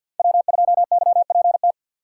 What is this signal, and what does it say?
Morse code, heard as a keyed tone, about 47 wpm, 700 Hz. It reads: W2YPT
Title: USA Call Signs - 6x Character Spacing - 50wpm